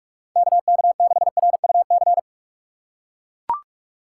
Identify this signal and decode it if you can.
Morse code, heard as a keyed tone, about 41 wpm, 700 Hz. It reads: KK6RUC